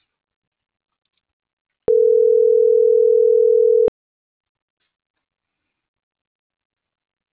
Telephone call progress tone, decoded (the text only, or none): ringback tone